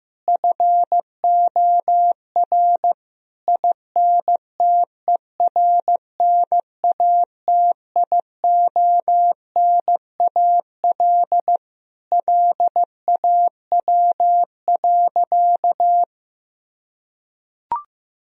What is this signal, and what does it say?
Morse code, heard as a keyed tone, about 15 wpm, 700 Hz. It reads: FOR INTERNATIONAL LAW.
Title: Sentences from Top 1000 Words - Review Part 4 - 15wpm